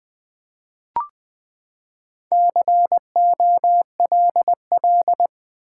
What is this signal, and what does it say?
Morse code, heard as a keyed tone, about 20 wpm, 700 Hz. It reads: COLL
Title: Build A Word - 4 Letters At A Time - 8-Letter Words - 20wpm